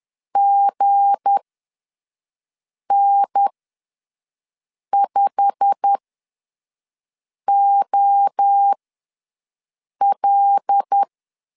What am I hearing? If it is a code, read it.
Morse code, heard as a keyed tone, about 10 wpm, spaced as for 5 wpm, 800 Hz. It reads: GN5OL
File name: LEZIONI / 9 2 ws 10wpm 4130